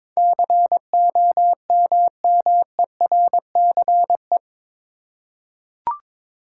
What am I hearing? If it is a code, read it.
Morse code, heard as a keyed tone, about 22 wpm, 700 Hz. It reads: COMMERCE
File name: practice / 22wpm / Eight Letter Words - 22wpm